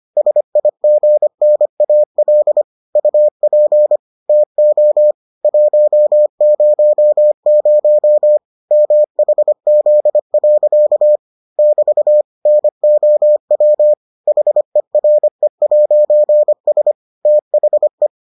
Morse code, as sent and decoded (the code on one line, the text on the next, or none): ... .. --. -. .- .-.. / ..- .--. / - --- / .---- ----- ----- / -- .... --.. .-.-.- / -...- / -. --- .-- / .... .-. .----. ... / - .... .
SIGNAL UP TO 100 MHZ. <BT> NOW HERE'S THE